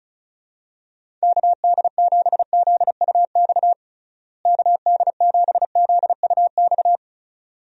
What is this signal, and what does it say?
Morse code, heard as a keyed tone, about 36 wpm, 700 Hz. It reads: KD7ZUX KD7ZUX
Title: USA Call Signs - 6 Characters - Repeated 2x - 35wpm